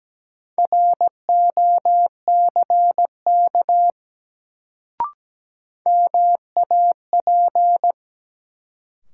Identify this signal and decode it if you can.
Morse code, heard as a keyed tone, about 17 wpm, 700 Hz. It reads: ROCK MAP
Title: Sentences from Top 400 Words - New Word Drill - 17wpm